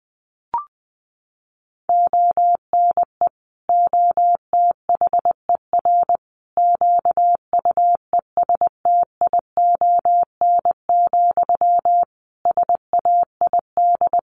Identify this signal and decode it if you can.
Morse code, heard as a keyed tone, about 20 wpm, 700 Hz. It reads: ONE OTHER QUESTION, SAID
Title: Sentences from Top 400 Words 20wpm